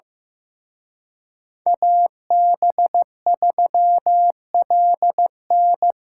Morse code, heard as a keyed tone, about 18 wpm, 700 Hz. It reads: AB3LN
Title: Speed Racing - USA 2x2 Call Signs 15wpm